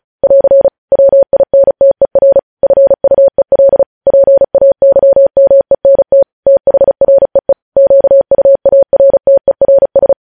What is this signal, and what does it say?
Morse code, heard as a keyed tone, about 36 wpm, 550 Hz. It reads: <AR> WINTER FUEL PAYMENT THREE QUARTERS
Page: news headlines in morse code at 35 wpm